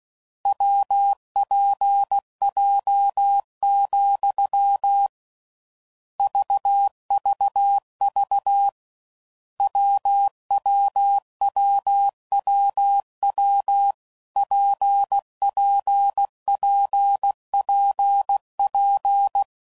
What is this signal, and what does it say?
Morse code, heard as a keyed tone, about 16 wpm, 800 Hz. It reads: WPJ, VVV WWWWW PPPPP
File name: Combinations / Difficult Combos / WPJ WPJ1